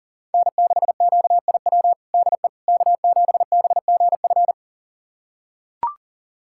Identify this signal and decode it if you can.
Morse code, heard as a keyed tone, about 40 wpm, 700 Hz. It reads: N6QIW DE K7BGF
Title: USA Callsign DE Callsign - 5 Characters - 40wpm